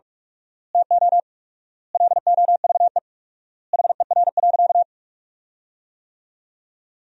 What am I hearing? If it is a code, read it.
Morse code, heard as a keyed tone, about 46 wpm, 700 Hz. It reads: TO LOVE HER.
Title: 4x Word-Spacing - Sentences from Top 400 Words - 45wpm